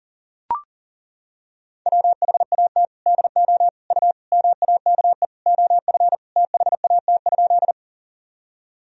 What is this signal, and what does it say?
Morse code, heard as a keyed tone, about 40 wpm, 700 Hz. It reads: WHAT DO U MAKE OF THAT?
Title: Sentences from Top 100 Words and Abbreviations 40wpm